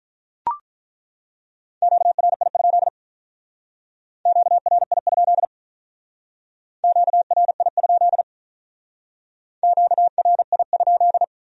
Morse code, heard as a keyed tone, about 43 wpm, 700 Hz. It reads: QRI? QRI? QRI? QRI?
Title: Speed-Racing Q-Codes - 35wpm